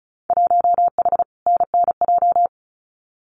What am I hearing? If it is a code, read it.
Morse code, heard as a keyed tone, about 34 wpm, 700 Hz. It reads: 1H NNJ
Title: Winter Field Day Exchanges 35wpm